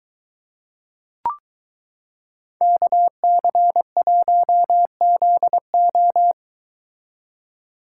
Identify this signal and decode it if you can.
Morse code, heard as a keyed tone, about 23 wpm, 700 Hz. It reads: KC1ZO